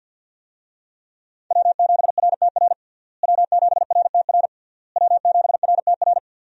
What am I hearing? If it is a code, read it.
Morse code, heard as a keyed tone, about 51 wpm, 700 Hz. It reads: W7RTR W7RTR W7RTR
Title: USA Call Signs - 5 Characters - Repeated 3x - 50wpm